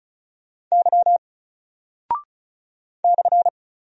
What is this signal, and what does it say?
Morse code, heard as a keyed tone, about 36 wpm, 700 Hz. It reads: Y /
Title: Single Character (Letter:Number:Punctuation:Prosign) Alt 35wpm